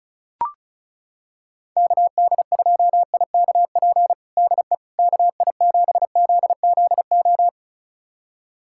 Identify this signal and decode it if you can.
Morse code, heard as a keyed tone, about 34 wpm, 700 Hz. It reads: KD2IKP DE KI7ZZO
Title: USA Callsign DE Callsign - 6 Characters - 35wpm